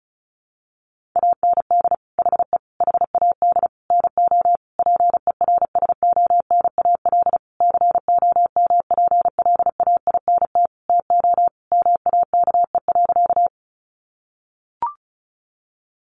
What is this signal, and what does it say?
Morse code, heard as a keyed tone, about 34 wpm, 700 Hz. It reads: AND HE HAD NO PERSONAL COMPLAINT TO MAKE.
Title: Sentences from Top 1400 Words 35wpm